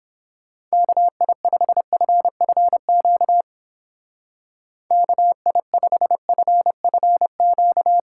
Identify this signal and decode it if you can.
Morse code, heard as a keyed tone, about 29 wpm, 700 Hz. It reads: KI5FFQ KI5FFQ